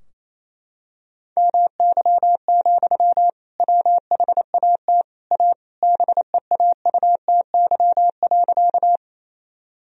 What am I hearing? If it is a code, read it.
Morse code, heard as a keyed tone, about 28 wpm, 700 Hz. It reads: MY, WHAT A BEAUTY.
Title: Sentences from Top 1000 Words - Review Part 2 - 28wpm